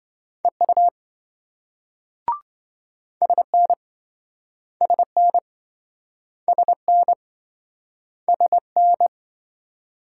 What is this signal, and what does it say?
Morse code, heard as a keyed tone, about 27 wpm, 700 Hz. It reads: EU SN SN SN SN